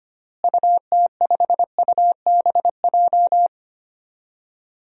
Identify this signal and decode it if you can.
Morse code, heard as a keyed tone, about 25 wpm, 700 Hz. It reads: UT5UBJ
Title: DXCC-100 Call Signs - 6 Characters - Repeated 3x - 25wpm